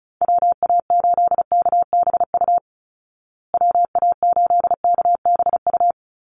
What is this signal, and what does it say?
Morse code, heard as a keyed tone, about 34 wpm, 700 Hz. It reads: WA8KBU WA8KBU
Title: Call Signs Repeated 2x - 35wpm